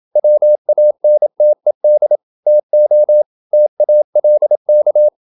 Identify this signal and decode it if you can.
Morse code, heard as a keyed tone, about 27 wpm, 600 Hz. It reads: WANTED TO TALK
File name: Never 0027wpm